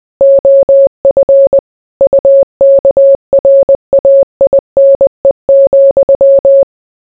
Morse code, heard as a keyed tone, about 20 wpm, 550 Hz. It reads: OF UKRAINE,